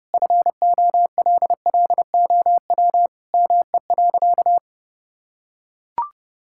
Morse code, heard as a keyed tone, about 30 wpm, 700 Hz. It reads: FOLLOW ME.
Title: Sentences from Top 1000 Words - Review Part 6 - 30wpm